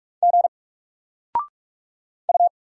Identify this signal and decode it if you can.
Morse code, heard as a keyed tone, about 46 wpm, 700 Hz. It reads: G U
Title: Single Letter-Number 45wpm